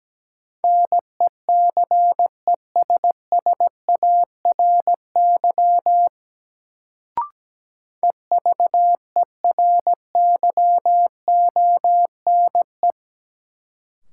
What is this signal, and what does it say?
Morse code, heard as a keyed tone, about 17 wpm, 700 Hz. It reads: NECESSARY EVERYONE
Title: Sentences from Top 800 Words - New Word Drill - 17wpm